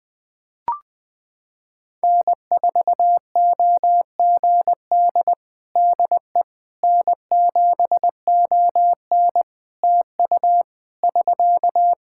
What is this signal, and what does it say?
Morse code, heard as a keyed tone, about 20 wpm, 700 Hz. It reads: N4OGD DE N7ON TU <SK>
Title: Practice QSO - Exchange 4 - Ending - 20wpm